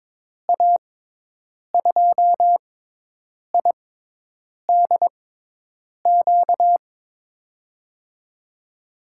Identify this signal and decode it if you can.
Morse code, heard as a keyed tone, about 22 wpm, spaced as for 8 wpm, 700 Hz. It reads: A2IDQ
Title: USA Call Signs - 6 Characters - 6x Character Spacing - 22wpm